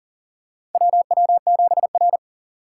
Morse code, heard as a keyed tone, about 40 wpm, 700 Hz. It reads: WW7R
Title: USA Call Signs - 4 Characters - 3x Character Spacing - 40wpm